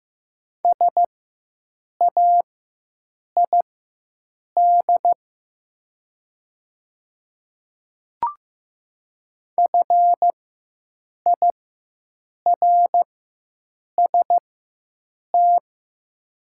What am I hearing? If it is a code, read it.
Morse code, heard as a keyed tone, about 15 wpm, spaced as for 7 wpm, 700 Hz. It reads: SAID FIRST
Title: 4x Character Spacing - Top 100 Words - 15wpm